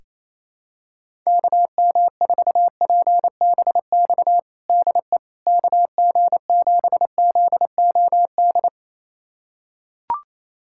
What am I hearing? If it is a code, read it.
Morse code, heard as a keyed tone, about 28 wpm, 700 Hz. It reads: KM4PBX DE KG7ZOD
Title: USA Callsign DE Callsign - 6 Characters - 28wpm